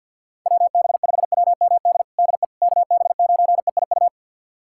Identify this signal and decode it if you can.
Morse code, heard as a keyed tone, about 52 wpm, 700 Hz. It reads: WB5WMD DE KB9EIU
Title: USA Callsign DE Callsign - 6 Characters - 50wpm